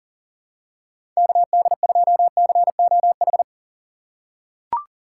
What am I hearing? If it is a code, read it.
Morse code, heard as a keyed tone, about 40 wpm, 700 Hz. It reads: KD2COH
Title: USA Call Signs - 6 Characters - 6x Character Spacing - 40wpm